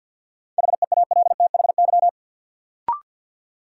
Morse code, heard as a keyed tone, about 48 wpm, 700 Hz. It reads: HEALTHY
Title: Sentences from Top 1000 Words - Review Part 5 - New Word Drill - 50wpm